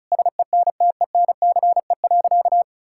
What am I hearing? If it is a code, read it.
Morse code, heard as a keyed tone, about 36 wpm, 700 Hz. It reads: SENTENCE.
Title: Sentences from Top 1000 Words - Review Part 6 - 35wpm